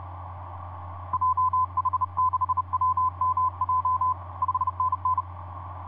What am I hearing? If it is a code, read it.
Morse code, heard as a keyed tone, about 32 wpm, 1020 Hz, with radio noise. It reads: JH6WMJ 5NN
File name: vp8dmn ja1ghh etc 30m cw 23jan2010 2351